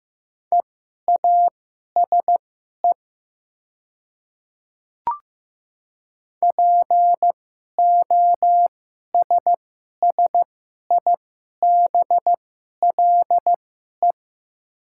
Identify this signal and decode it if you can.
Morse code, heard as a keyed tone, about 15 wpm, spaced as for 11 wpm, 700 Hz. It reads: EASE POSSIBLE
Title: 2x Character Spacing - Top 500 Words - 15wpm